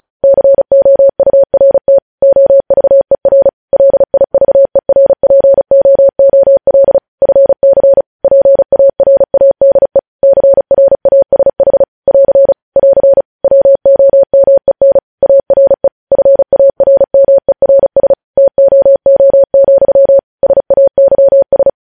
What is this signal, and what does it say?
Morse code, heard as a keyed tone, about 36 wpm, 550 Hz. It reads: COURT OVER LIVERPOOL FC PARADE CRASH <AR> <AR> WOMEN ARE FARMERS TOO, SAYS
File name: podcast / News headlines-1x-35x35